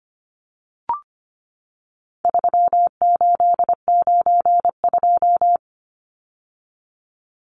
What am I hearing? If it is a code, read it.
Morse code, heard as a keyed tone, about 25 wpm, 700 Hz. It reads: 3892
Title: Numbers - 4-Digit 25wpm